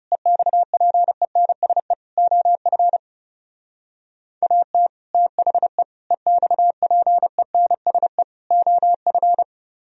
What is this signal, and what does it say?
Morse code, heard as a keyed tone, about 37 wpm, 700 Hz. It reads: EXPENSE OF AT THE EXPENSE OF
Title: Speed-Racing - Sets of 4 Words 30wpm